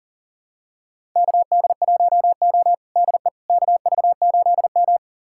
Morse code, heard as a keyed tone, about 37 wpm, 700 Hz. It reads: KD1O DE KV8M
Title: USA Callsign DE Callsign - 4 Characters - 40wpm